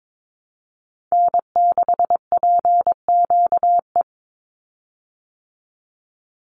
Morse code, heard as a keyed tone, about 22 wpm, 700 Hz. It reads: N6PQE
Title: USA Call Signs 22wpm